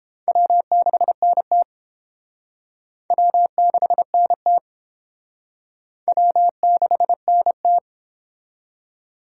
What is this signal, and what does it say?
Morse code, heard as a keyed tone, about 32 wpm, 700 Hz. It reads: W6NT W6NT W6NT